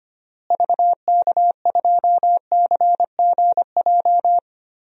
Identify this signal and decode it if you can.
Morse code, heard as a keyed tone, about 25 wpm, 700 Hz. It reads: VK2CGJ
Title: DXCC-100 Call Signs - 6 Characters - Repeated 3x - 25wpm